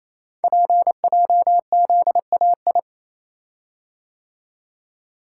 Morse code, heard as a keyed tone, about 27 wpm, 700 Hz. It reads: PJZAI
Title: Code Groups - 5 Letters - 28wpm